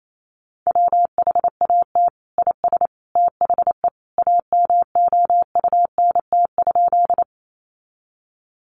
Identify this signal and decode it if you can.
Morse code, heard as a keyed tone, about 28 wpm, 700 Hz. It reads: WHAT IS THE AMOUNT?
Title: Sentences from Top 700 Words 28wpm